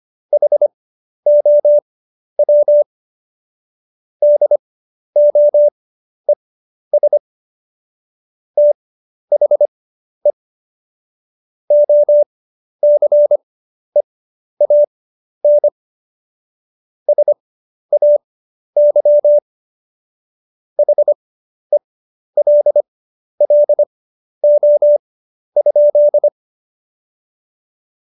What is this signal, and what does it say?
Morse code, heard as a keyed tone, about 25 wpm, spaced as for 11 wpm, 590 Hz. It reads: HOW DOES THE OCEAN SAY HELLO?
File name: Jokes-25-char-10-eff